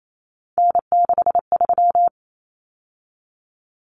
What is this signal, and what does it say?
Morse code, heard as a keyed tone, about 28 wpm, 700 Hz. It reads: N63
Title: Common Cut Numbers - 3-Digit 28wpm